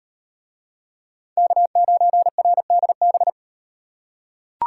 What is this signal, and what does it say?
Morse code, heard as a keyed tone, about 38 wpm, 700 Hz. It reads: K9RDB